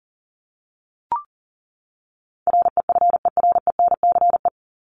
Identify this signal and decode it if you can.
Morse code, heard as a keyed tone, about 40 wpm, 700 Hz. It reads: REFERENCE
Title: Top 1001-1250 Words 40wpm